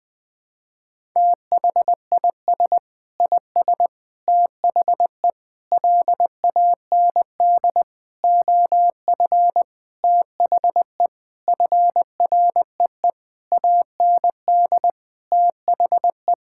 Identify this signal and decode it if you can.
Morse code, heard as a keyed tone, about 20 wpm, 700 Hz. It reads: THIS IS THE LAND OF THE FREE AND THE
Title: Sentences with 9-Word Phrases - 20wpm